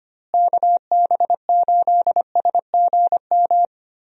Morse code, heard as a keyed tone, about 25 wpm, 700 Hz. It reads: KB8SGM
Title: USA Call Signs - 6 Characters - Repeated 3x - 25wpm